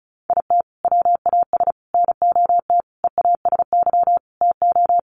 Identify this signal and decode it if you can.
Morse code, heard as a keyed tone, about 34 wpm, 700 Hz. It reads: IT WAS NOT EASY TO
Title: Sentences from Top 1500 Words 35wpm